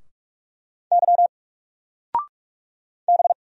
Morse code, heard as a keyed tone, about 44 wpm, 700 Hz. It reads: Y B
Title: Single Character (Letter:Number:Punctuation:Prosign) Alt 45wpm